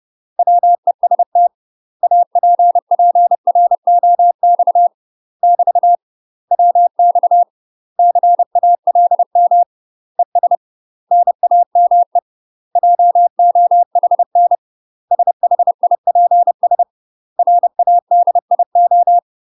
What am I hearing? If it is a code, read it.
Morse code, heard as a keyed tone, about 30 wpm, 700 Hz. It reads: WEST APPROX <BT> WX CALM ES NAME JOHN SHIPS RADIO
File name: UK MASTER QSOs TRAINING/30WPM